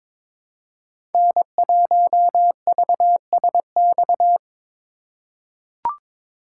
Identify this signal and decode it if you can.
Morse code, heard as a keyed tone, about 22 wpm, 700 Hz. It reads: N1VSX
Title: USA Call Signs - 5 Characters - 3x Character Spacing - 22wpm